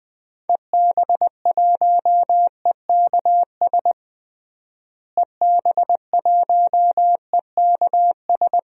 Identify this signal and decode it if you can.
Morse code, heard as a keyed tone, about 20 wpm, 700 Hz. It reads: EB1EKS EB1EKS
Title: DXCC-100 Call Signs - 6 Characters - Repeated 2x - 20wpm